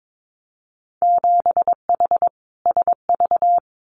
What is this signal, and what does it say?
Morse code, heard as a keyed tone, about 22 wpm, 700 Hz. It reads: 7H SV
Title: Winter Field Day Exchanges Encore 22wpm